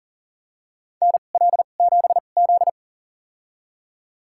Morse code, heard as a keyed tone, about 40 wpm, spaced as for 29 wpm, 690 Hz. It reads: NL7Z
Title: USA Call Signs - 4 Characters - 2x Character Spacing - 40wpm